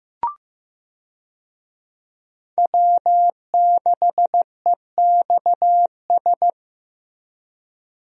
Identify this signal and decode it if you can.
Morse code, heard as a keyed tone, about 15 wpm, 700 Hz. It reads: W6EXS
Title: USA Call Signs - 5 Characters - Repeated 3x - 15wpm